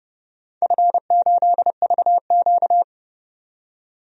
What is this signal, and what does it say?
Morse code, heard as a keyed tone, about 30 wpm, 700 Hz. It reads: F8VQ
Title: DXCC-100 Call Signs - 4 Characters - 30wpm